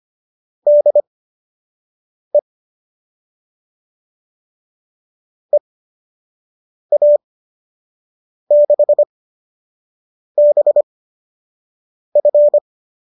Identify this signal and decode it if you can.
Morse code, heard as a keyed tone, about 25 wpm, spaced as for 6 wpm, 590 Hz. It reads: DE EA6BF